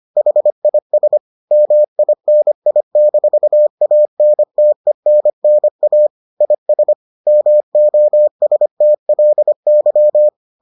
Morse code, heard as a keyed tone, about 25 wpm, 600 Hz. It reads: HIS MINI-ANTENNA IS MOSTLY